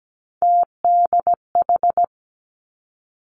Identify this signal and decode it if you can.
Morse code, heard as a keyed tone, about 17 wpm, 700 Hz. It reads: TDH
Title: Three Challenging Letter-Number Combinations 17wpm